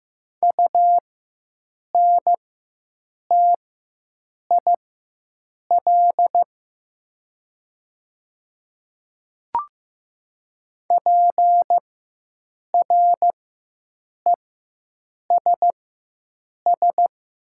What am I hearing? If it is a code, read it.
Morse code, heard as a keyed tone, about 15 wpm, spaced as for 7 wpm, 700 Hz. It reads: UNTIL PRESS
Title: 4x Character Spacing - Top 300 Words - 15wpm